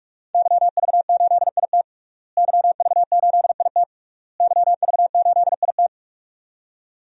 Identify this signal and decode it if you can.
Morse code, heard as a keyed tone, about 46 wpm, 700 Hz. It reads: YV8IT YV8IT YV8IT
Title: DXCC-100 Call Signs - 5 Characters - Repeated 3x - 45wpm